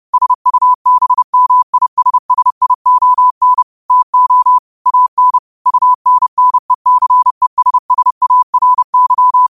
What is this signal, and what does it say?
Morse code, heard as a keyed tone, about 30 wpm, 1000 Hz. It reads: SUBMISSION TO AN UNNECESSARY
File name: qotdfiles / 30wpm